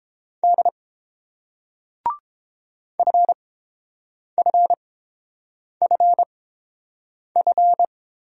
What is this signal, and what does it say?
Morse code, heard as a keyed tone, about 30 wpm, spaced as for 7 wpm, 700 Hz. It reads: D FFFF